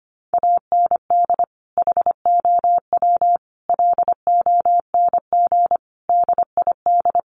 Morse code, heard as a keyed tone, about 25 wpm, 700 Hz. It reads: AND HOW LONG DID
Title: Sentences from Top 1500 Words 25wpm